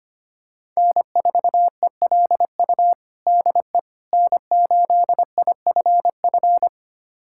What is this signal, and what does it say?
Morse code, heard as a keyed tone, about 25 wpm, 700 Hz. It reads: N4ELU DE N8IFF
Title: USA Callsign DE Callsign - 5 Characters - 25wpm